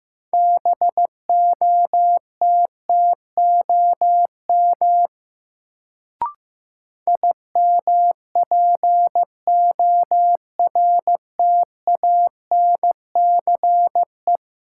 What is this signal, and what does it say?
Morse code, heard as a keyed tone, about 15 wpm, 700 Hz. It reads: BOTTOM IMPORTANCE
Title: Sets of 3 Words - New Word Drill - 15wpm